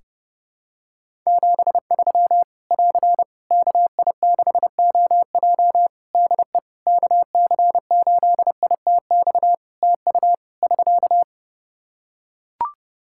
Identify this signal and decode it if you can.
Morse code, heard as a keyed tone, about 30 wpm, 700 Hz. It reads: 73 <AR> KI6OJ DE KC8ITX TU <SK>
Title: K1USN SST QSO Practice 30wpm